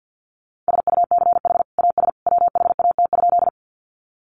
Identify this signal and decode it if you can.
Morse code, heard as a keyed tone, about 50 wpm, 700 Hz. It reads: SUCH AS WHAT?
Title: Sentences from Top 200 Words 50wpm